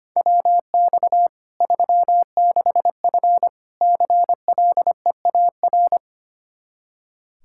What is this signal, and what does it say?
Morse code, heard as a keyed tone, about 25 wpm, 700 Hz. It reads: WX 36F CLEAR
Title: Practice QSO - Exchange 2 - Info - 25wpm